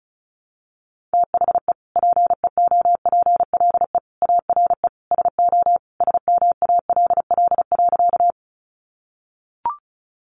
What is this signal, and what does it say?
Morse code, heard as a keyed tone, about 34 wpm, 700 Hz. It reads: THE PEOPLE ARE SO SMALL.
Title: Additional Easy Phrases 35wpm